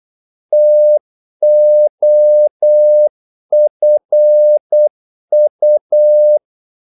Morse code, heard as a keyed tone, about 8 wpm, 600 Hz. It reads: TOFU
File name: Korte ord DK 8wpm